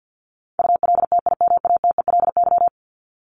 Morse code, heard as a keyed tone, about 50 wpm, 700 Hz. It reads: ULTIMATELY
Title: Top 1001-1250 Words 50wpm